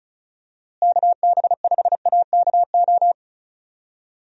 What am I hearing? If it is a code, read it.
Morse code, heard as a keyed tone, about 34 wpm, 700 Hz. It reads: KB5AKO
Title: USA Call Signs - 6 Characters - Repeated 2x - 35wpm